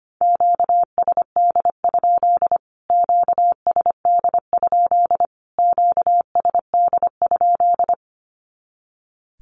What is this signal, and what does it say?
Morse code, heard as a keyed tone, about 25 wpm, 700 Hz. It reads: QSD? QSD? QSD?
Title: Q-Codes Repeated 3x 25wpm